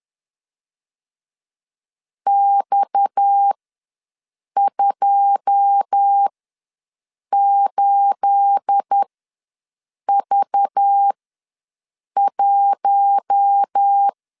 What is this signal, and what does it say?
Morse code, heard as a keyed tone, about 10 wpm, spaced as for 6 wpm, 800 Hz. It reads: X28V1
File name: LEZIONI / 9 286 ws 10wpm 4032